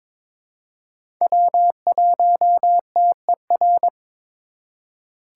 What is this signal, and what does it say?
Morse code, heard as a keyed tone, about 22 wpm, 700 Hz. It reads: W1TER
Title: K1USN SST QSO Practice - 22wpm